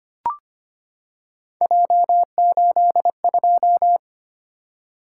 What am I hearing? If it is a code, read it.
Morse code, heard as a keyed tone, about 25 wpm, 700 Hz. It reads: J82
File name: Three Letter-Number Combinations - 25wpm